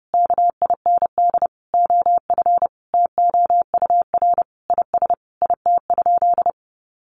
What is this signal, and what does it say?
Morse code, heard as a keyed tone, about 28 wpm, 700 Hz. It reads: KIND OF TOUR IS IT?
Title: Sentences from Top 1400 Words 30wpm